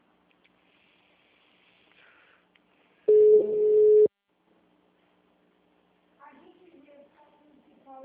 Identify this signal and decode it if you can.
Ringback tone (Europe, 425 Hz), with speech.
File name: out- -2003- - - .20451